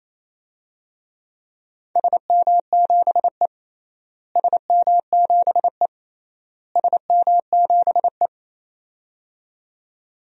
Morse code, heard as a keyed tone, about 28 wpm, 700 Hz. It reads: SM7E SM7E SM7E
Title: DXCC-100 Call Signs - 4 Characters - Repeated 3x - 28wpm